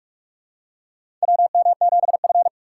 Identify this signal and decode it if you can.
Morse code, heard as a keyed tone, about 46 wpm, 700 Hz. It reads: WM7F
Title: USA Call Signs - 2x Character Spacing - 45wpm